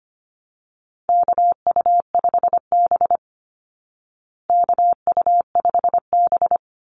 Morse code, heard as a keyed tone, about 25 wpm, 700 Hz. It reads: KU5B KU5B
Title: Call Signs Repeated 2x - 25wpm